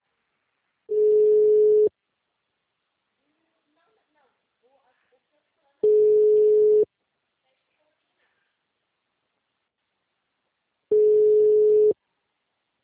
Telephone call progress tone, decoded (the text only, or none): ringback tone